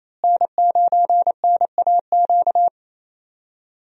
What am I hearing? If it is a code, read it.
Morse code, heard as a keyed tone, about 28 wpm, 700 Hz. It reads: N9NAQ
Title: USA Call Signs - 5 Characters - Repeated 3x - 28wpm